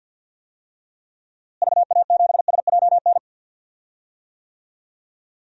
Morse code, heard as a keyed tone, about 53 wpm, 700 Hz. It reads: VA7SJN
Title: DXCC-100 Call Signs - 6 Characters - Repeated 3x - 50wpm